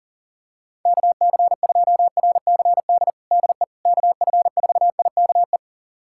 Morse code, heard as a keyed tone, about 40 wpm, 700 Hz. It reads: KC2RCD DE KF4IKE
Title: USA Callsign DE Callsign 40wpm